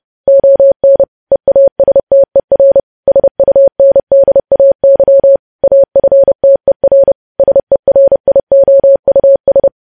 Morse code, heard as a keyed tone, about 30 wpm, 550 Hz. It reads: ON EASTER SUNDAY AFTER SERIOUS
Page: news headlines in morse code at 30 wpm